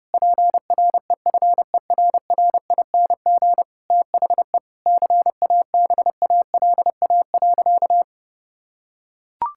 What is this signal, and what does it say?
Morse code, heard as a keyed tone, about 32 wpm, 700 Hz. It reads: PREFERRING THE CABALA.
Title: Pangram Challenge v2 - 30wpm